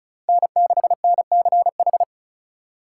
Morse code, heard as a keyed tone, about 36 wpm, 700 Hz. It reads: N6NCH
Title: USA Call Signs - 5 Characters - 4x Character Spacing - 35wpm